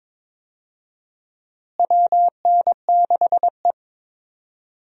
Morse code, heard as a keyed tone, about 22 wpm, 700 Hz. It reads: WN6E
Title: USA Call Signs - 4 Characters - Repeated 2x - 22wpm